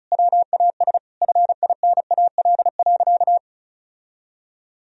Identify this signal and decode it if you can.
Morse code, heard as a keyed tone, about 36 wpm, 700 Hz. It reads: WAS FINAL.
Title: Sentences from Top 1000 Words - Review Part 2 - 35wpm